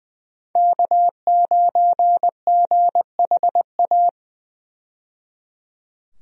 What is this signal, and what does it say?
Morse code, heard as a keyed tone, about 20 wpm, 700 Hz. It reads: K9GHA
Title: USA Call Signs - 5 Characters - 20wpm